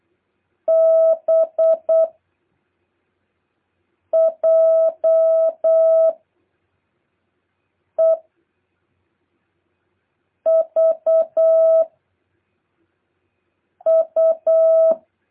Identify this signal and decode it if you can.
Morse code, heard as a keyed tone, about 8 wpm, 650 Hz. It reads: B J E V U
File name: jh8cbh / 2023cw